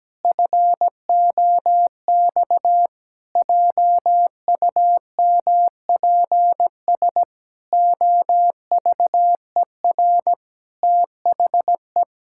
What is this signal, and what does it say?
Morse code, heard as a keyed tone, about 17 wpm, 700 Hz. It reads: FOX JUMPS OVER THE
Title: Pangram Challenge v2 - 17wpm